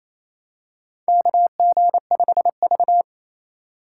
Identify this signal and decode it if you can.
Morse code, heard as a keyed tone, about 28 wpm, 700 Hz. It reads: KG5V
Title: K1USN SST QSO Practice - 28wpm